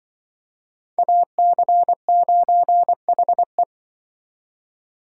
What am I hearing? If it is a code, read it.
Morse code, heard as a keyed tone, about 24 wpm, 700 Hz. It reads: AC9HE
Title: Speed Racing - USA 2x2 Call Signs 20wpm